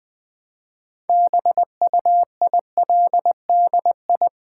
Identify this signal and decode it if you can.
Morse code, heard as a keyed tone, about 20 wpm, 700 Hz. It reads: BUILDI
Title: Build A Word - 2 Letters At A Time - 8-Letter Words - 20wpm